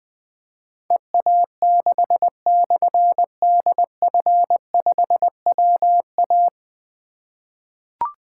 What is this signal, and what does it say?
Morse code, heard as a keyed tone, about 20 wpm, 700 Hz. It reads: EA6/DF5WA
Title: Difficult Call Signs 20wpm